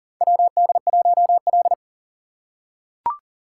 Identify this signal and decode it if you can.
Morse code, heard as a keyed tone, about 40 wpm, 700 Hz. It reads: WD1L
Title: USA Call Signs - 4 Characters - 6x Character Spacing - 40wpm